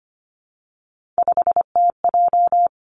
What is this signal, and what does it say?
Morse code, heard as a keyed tone, about 25 wpm, 700 Hz. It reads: 5TJ
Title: Three Challenging Letter-Number Combinations 25wpm